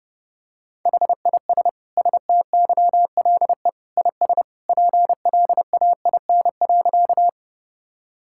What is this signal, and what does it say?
Morse code, heard as a keyed tone, about 30 wpm, 700 Hz. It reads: HIS STYLE IS PLAIN.
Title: Sentences from Top 1000 Words - Review Part 5 - 30wpm